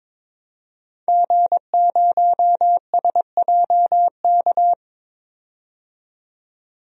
Morse code, heard as a keyed tone, about 22 wpm, 700 Hz. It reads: G0SJK
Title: DXCC-100 Call Signs - 5 Characters - 22wpm